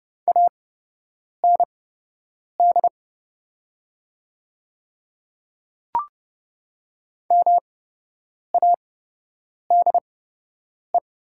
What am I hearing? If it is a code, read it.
Morse code, heard as a keyed tone, about 30 wpm, spaced as for 8 wpm, 700 Hz. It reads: AND MADE
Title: 8x Character Spacing - Top 200 Words - 30wpm